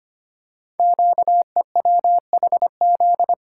Morse code, heard as a keyed tone, about 25 wpm, 700 Hz. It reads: QEWHZ
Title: Code Groups - 5 Letter:Numbers - 25wpm